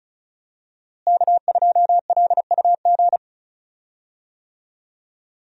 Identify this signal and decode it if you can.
Morse code, heard as a keyed tone, about 36 wpm, 700 Hz. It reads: K2LUG
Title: USA Call Signs - 5 Characters - Repeated 3x - 35wpm